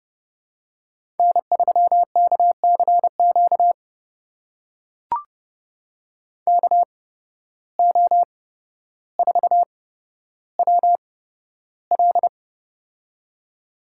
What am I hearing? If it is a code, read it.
Morse code, heard as a keyed tone, about 30 wpm, spaced as for 13 wpm, 700 Hz. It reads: N3KCQ KO4WL
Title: USA Call Signs - 5 Characters - 8x Character Spacing - 30wpm